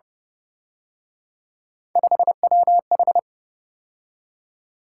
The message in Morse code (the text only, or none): 5WH